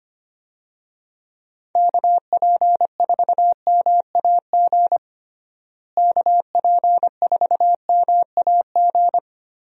Morse code, heard as a keyed tone, about 25 wpm, 700 Hz. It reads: KP4MAG KP4MAG
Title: USA Call Signs - 6 Characters - Repeated 3x - 25wpm